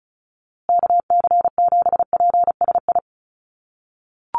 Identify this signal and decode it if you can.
Morse code, heard as a keyed tone, about 34 wpm, 700 Hz. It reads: KC7PSI
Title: USA Call Signs Repeated 3x 35wpm